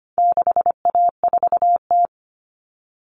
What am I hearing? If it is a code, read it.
Morse code, heard as a keyed tone, about 25 wpm, 700 Hz. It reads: 6A4T
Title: All Cut Numbers - 4-Digit 25wpm